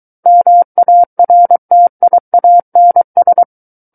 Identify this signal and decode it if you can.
Morse code, heard as a keyed tone, about 23 wpm, 700 Hz. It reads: MARTIANS
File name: Book One / 14-Chapter 14 - 23 WPM